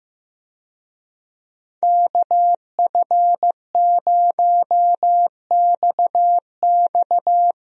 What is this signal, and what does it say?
Morse code, heard as a keyed tone, about 15 wpm, 700 Hz. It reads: KF0XX
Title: USA Call Signs - 5 Characters - Repeated 2x - 15wpm